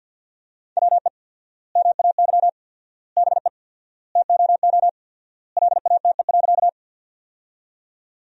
Morse code, heard as a keyed tone, about 50 wpm, 700 Hz. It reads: WE MAY BE TOO LATE.
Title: 4x Word-Spacing - Sentences from Top 300 Words - 50wpm